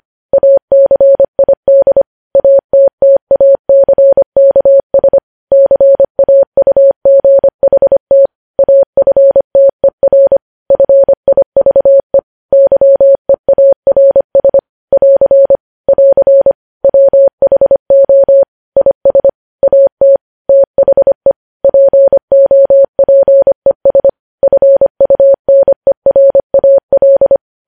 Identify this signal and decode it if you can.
Morse code, heard as a keyed tone, about 25 wpm, 550 Hz. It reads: ACID ATTACKS CAUGHT AFTER FIVE YEARS <AR> <AR> WHO IS AT THE POPES FUNERAL
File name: podcast / News headlines-1x-25x25